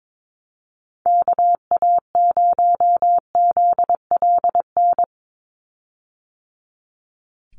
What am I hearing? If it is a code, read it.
Morse code, heard as a keyed tone, about 22 wpm, 700 Hz. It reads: KA0ZLN
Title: USA Call Signs 22wpm